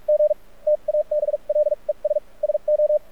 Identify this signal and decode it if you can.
Morse code, heard as a keyed tone, about 46 wpm, spaced as for 36 wpm, 600 Hz. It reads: G TABLES SO
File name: archived / CRSnet